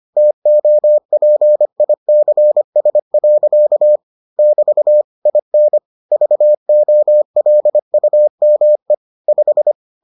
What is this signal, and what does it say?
Morse code, heard as a keyed tone, about 25 wpm, 600 Hz. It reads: TOPICS. <BT> IN VOLUME 5